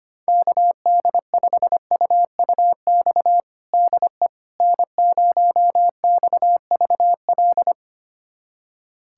Morse code, heard as a keyed tone, about 25 wpm, 700 Hz. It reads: KD5UUX DE N0XVL
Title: USA Callsign DE Callsign 25wpm